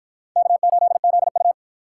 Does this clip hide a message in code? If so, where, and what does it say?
Yes: Morse code from 0.4 to 1.5 s, heard as a keyed tone, about 53 wpm, 700 Hz. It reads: K8ZU